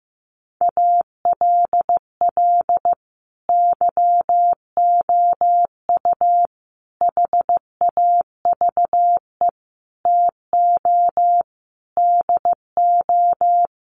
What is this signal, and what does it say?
Morse code, heard as a keyed tone, about 15 wpm, 700 Hz. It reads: ALL YOU HAVE TO DO
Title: Sets of 5 Words 15wpm